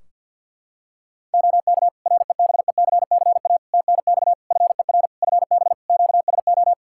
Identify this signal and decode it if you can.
Morse code, heard as a keyed tone, about 52 wpm, 700 Hz. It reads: OK REBECCA TNX FER FB QSO
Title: Practice QSO - Exchange 4 - Ending - 50wpm